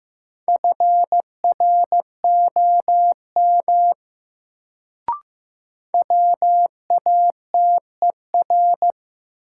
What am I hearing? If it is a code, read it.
Morse code, heard as a keyed tone, about 15 wpm, 700 Hz. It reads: FROM WATER
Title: Sentences from Top 100 Words - New Word Drill - 15wpm